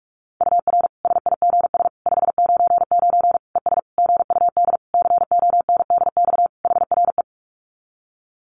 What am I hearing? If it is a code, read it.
Morse code, heard as a keyed tone, about 44 wpm, 700 Hz. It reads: UR SIGS 599 ES GUD CONDX HRE
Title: Common QSO Phrases 45wpm